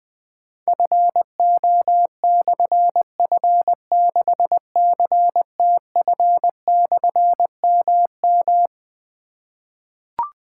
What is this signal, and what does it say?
Morse code, heard as a keyed tone, about 20 wpm, 700 Hz. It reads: FO/F6CTF/MM
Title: Difficult Call Signs 20wpm